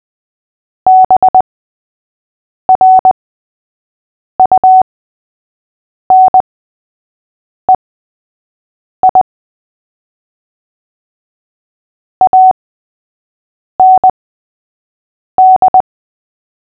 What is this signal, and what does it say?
Morse code, heard as a keyed tone, about 20 wpm, spaced as for 6 wpm, 750 Hz. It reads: BRUNEI AND
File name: cw / fists / Short-Story-3-6wpm